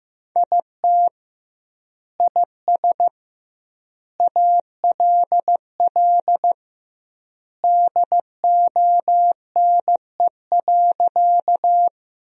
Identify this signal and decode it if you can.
Morse code, heard as a keyed tone, about 15 wpm, 700 Hz. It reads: IT IS ALL DONE.
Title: 2x Word-Spacing - Sentences from Top 500 Words - 15wpm